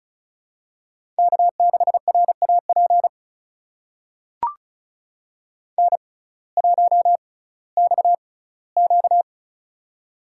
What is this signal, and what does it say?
Morse code, heard as a keyed tone, about 36 wpm, spaced as for 19 wpm, 700 Hz. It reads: K6RAP N1XQ
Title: USA Call Signs - 6x Character Spacing - 35wpm